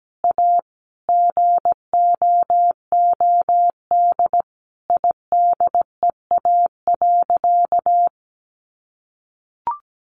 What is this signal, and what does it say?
Morse code, heard as a keyed tone, about 17 wpm, 700 Hz. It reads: A GOOD IDEA.
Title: Sentences from Top 600 Words 17wpm